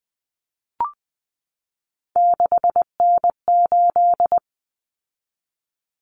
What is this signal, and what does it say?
Morse code, heard as a keyed tone, about 20 wpm, 700 Hz. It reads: 6N8
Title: Common Cut Numbers - 3-Digit 20wpm